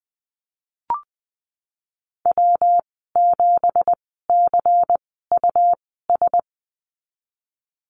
Morse code, heard as a keyed tone, about 20 wpm, spaced as for 15 wpm, 700 Hz. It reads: W7CUS
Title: 193 Lesson 31g - Letter v - Call Signs 20wpm